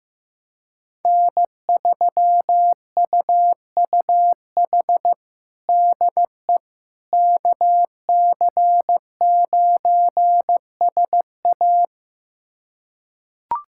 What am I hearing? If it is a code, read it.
Morse code, heard as a keyed tone, about 15 wpm, 700 Hz. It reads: N3UUH DE KC9SA
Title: USA Callsign DE Callsign - 5 Characters - 15wpm